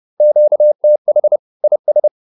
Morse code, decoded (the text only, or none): QTH IS